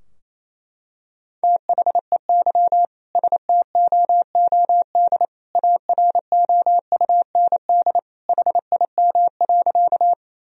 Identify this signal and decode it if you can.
Morse code, heard as a keyed tone, about 28 wpm, 700 Hz. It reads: THEY STOOD AROUND HIM.
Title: Sentences from Top 1000 Words - Review Part 3 - 28wpm